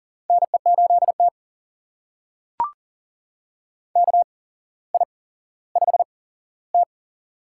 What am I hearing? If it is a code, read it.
Morse code, heard as a keyed tone, about 40 wpm, spaced as for 17 wpm, 700 Hz. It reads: NE8T KI5T
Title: USA Call Signs - 4 Characters - 8x Character Spacing - 40wpm